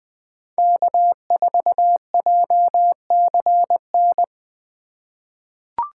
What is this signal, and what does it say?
Morse code, heard as a keyed tone, about 20 wpm, 700 Hz. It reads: K4JCN
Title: USA Call Signs - 5 Characters - 3x Character Spacing - 20wpm